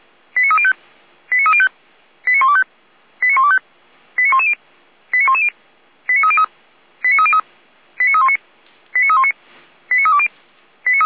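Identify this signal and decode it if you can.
Five-tone selective call (ZVEI), heard as quick runs of five tones, tones of about 70 ms. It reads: 89305 89305 89125 89125 89110 89110 89303 89303 89319 89319 89320
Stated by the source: Five-tone ZVEI selective call received on an SDR.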